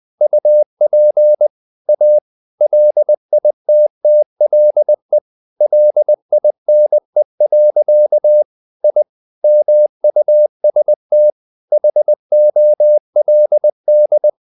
Morse code, heard as a keyed tone, about 20 wpm, 600 Hz. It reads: UP A LITTLE LINE. I MUST HOLD
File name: the old man and the sea part 2 20wpm